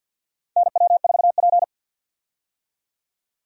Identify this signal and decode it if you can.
Morse code, heard as a keyed tone, about 47 wpm, 700 Hz. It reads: NW4P
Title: USA Call Signs - 4 Characters - 50wpm